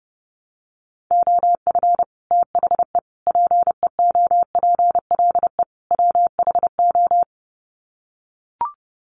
Morse code, heard as a keyed tone, about 30 wpm, 700 Hz. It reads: OF THE PEOPLE WHO
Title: Sets of 4 Words 30wpm